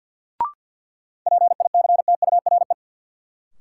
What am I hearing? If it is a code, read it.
Morse code, heard as a keyed tone, about 53 wpm, 700 Hz. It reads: PICTURE
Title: Sentences from Top 500 Words - New Word Drill - 50wpm